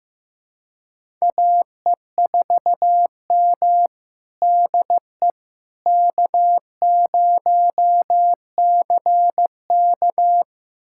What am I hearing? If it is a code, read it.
Morse code, heard as a keyed tone, about 16 wpm, 700 Hz. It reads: AE4M DE K0CK
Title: USA Callsign DE Callsign - 4 Characters - 15wpm